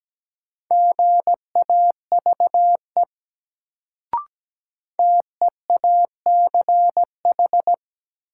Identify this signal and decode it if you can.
Morse code, heard as a keyed tone, about 17 wpm, 700 Hz. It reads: GAVE TEACH
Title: Sentences from Top 500 Words - New Word Drill - 17wpm